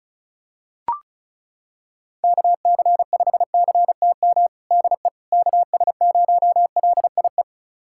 Morse code, heard as a keyed tone, about 34 wpm, 700 Hz. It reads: KC5CTM DE KS0LIE
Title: USA Callsign DE Callsign - 6 Characters - 35wpm